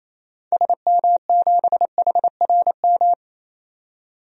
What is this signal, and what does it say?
Morse code, heard as a keyed tone, about 28 wpm, 700 Hz. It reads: SM7HRM
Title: DXCC-100 Call Signs - 6 Characters - 28wpm